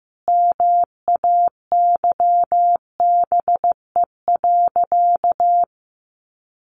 Morse code, heard as a keyed tone, about 15 wpm, 700 Hz. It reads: MAYBE.
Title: Sentences from Top 1000 Words 15wpm